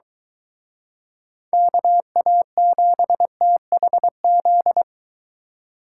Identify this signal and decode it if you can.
Morse code, heard as a keyed tone, about 23 wpm, 700 Hz. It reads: KA7THZ